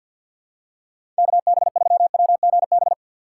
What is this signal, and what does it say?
Morse code, heard as a keyed tone, about 46 wpm, 700 Hz. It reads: KB3WGB
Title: USA Call Signs - 6 Characters - 3x Character Spacing - 50wpm